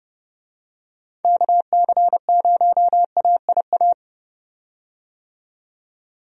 Morse code, heard as a keyed tone, about 30 wpm, 700 Hz. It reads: KC0AIA
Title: USA Call Signs - 6 Characters - 30wpm